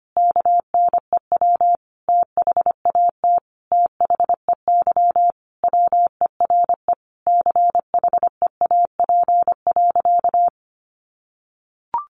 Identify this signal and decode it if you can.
Morse code, heard as a keyed tone, about 25 wpm, 700 Hz. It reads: KNEW THAT THEY WERE CHEAP.
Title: Sentences from Top 1200 Words 25wpm